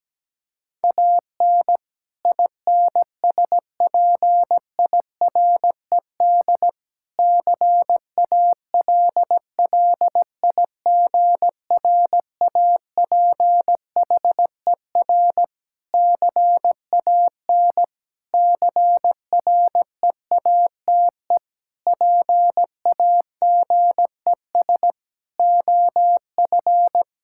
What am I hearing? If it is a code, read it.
Morse code, heard as a keyed tone, about 17 wpm, 700 Hz. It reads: AN INSPIRED CALLIGRAPHER CAN CREATE PAGES OF
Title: Pangram Challenge v2 - 17wpm